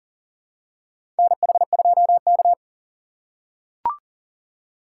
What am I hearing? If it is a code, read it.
Morse code, heard as a keyed tone, about 40 wpm, 700 Hz. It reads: NH2K
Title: USA Call Signs - 4 Characters - 6x Character Spacing - 40wpm